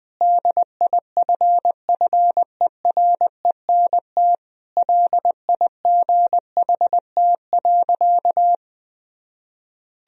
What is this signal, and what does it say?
Morse code, heard as a keyed tone, about 21 wpm, 700 Hz. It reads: DIFFERENT LIGHT.
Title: Sentences from Top 1000 Words - Review Part 3 - 20wpm